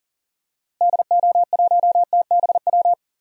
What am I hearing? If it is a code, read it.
Morse code, heard as a keyed tone, about 40 wpm, 700 Hz. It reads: DO1TBW
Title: DXCC-100 Call Signs - 6 Characters - Repeated 2x - 40wpm